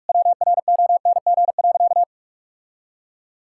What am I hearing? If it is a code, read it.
Morse code, heard as a keyed tone, about 46 wpm, 690 Hz. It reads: WRONG.
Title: Sentences from Top 1000 Words - Review Part 5 - 45wpm